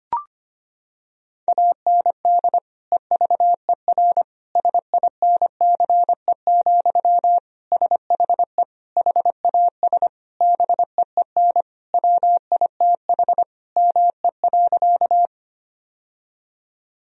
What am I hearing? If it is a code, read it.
Morse code, heard as a keyed tone, about 25 wpm, 700 Hz. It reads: AND EVER SINCE, SHE HAS BEEN WITH ME.
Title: Sentences from Top 1000 Words - Review Part 1 - 25wpm